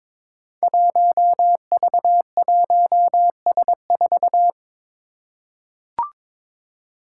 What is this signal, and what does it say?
Morse code, heard as a keyed tone, about 22 wpm, 700 Hz. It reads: 1V1S4
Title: Five Challenging Letter-Number Combinations 22wpm